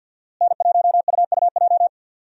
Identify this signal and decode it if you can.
Morse code, heard as a keyed tone, about 48 wpm, 700 Hz. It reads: N1UUJ
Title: USA Call Signs - 5 Characters - Repeated 3x - 50wpm